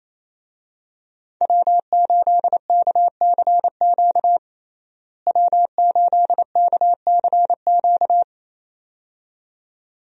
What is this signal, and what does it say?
Morse code, heard as a keyed tone, about 28 wpm, 700 Hz. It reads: W8KCQ W8KCQ
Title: USA Call Signs - 5 Characters - Repeated 2x - 28wpm